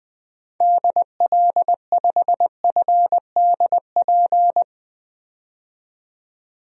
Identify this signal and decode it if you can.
Morse code, heard as a keyed tone, about 20 wpm, 700 Hz. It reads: DL5FDP
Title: DXCC-100 Call Signs - 6 Characters - 20wpm